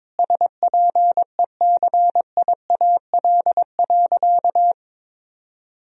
Morse code, heard as a keyed tone, about 22 wpm, 700 Hz. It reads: SPECIAL.
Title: Sentences from Top 1000 Words - Review Part 3 - 22wpm